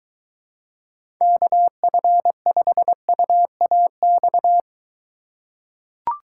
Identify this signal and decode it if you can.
Morse code, heard as a keyed tone, about 23 wpm, 700 Hz. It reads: KF5UAX